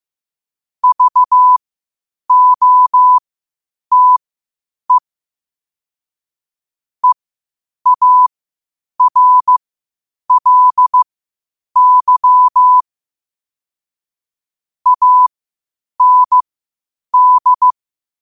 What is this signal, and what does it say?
Morse code, heard as a keyed tone, about 15 wpm, spaced as for 8 wpm, 1000 Hz. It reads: VOTE EARLY AND
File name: qotdfiles / 8wpm